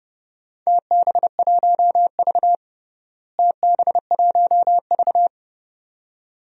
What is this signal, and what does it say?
Morse code, heard as a keyed tone, about 30 wpm, 700 Hz. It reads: TB1V TB1V
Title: DXCC-100 Call Signs - 4 Characters - Repeated 2x - 30wpm